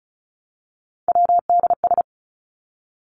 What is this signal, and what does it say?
Morse code, heard as a keyed tone, about 34 wpm, 700 Hz. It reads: WDS
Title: Abbreviations 35wpm